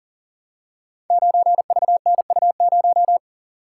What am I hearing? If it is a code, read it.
Morse code, heard as a keyed tone, about 40 wpm, 700 Hz. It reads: 9VNU0
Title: Code Groups - 5 Letter:Numbers - 40wpm